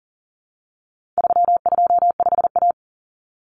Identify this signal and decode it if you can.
Morse code, heard as a keyed tone, about 40 wpm, 700 Hz. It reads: 325A
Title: Common Cut Numbers - 4-Digit 40wpm